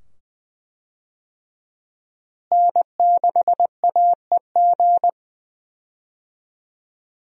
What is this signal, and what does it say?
Morse code, heard as a keyed tone, about 20 wpm, 700 Hz. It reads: N6AEG
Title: USA Call Signs - 5 Characters - Repeated 2x - 20wpm